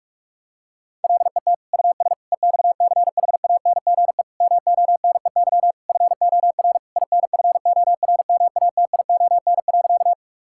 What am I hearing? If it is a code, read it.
Morse code, heard as a keyed tone, about 46 wpm, 700 Hz. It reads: LET US EXCHANGE MONEY FOR INFORMATION.
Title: Sentences from Top 1000 Words - Review Part 6 - 45wpm